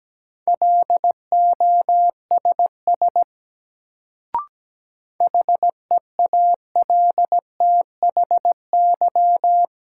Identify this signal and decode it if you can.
Morse code, heard as a keyed tone, about 17 wpm, 700 Hz. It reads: LOSS HEALTHY
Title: Sentences from Top 1400 Words - New Word Drill Alt - 17wpm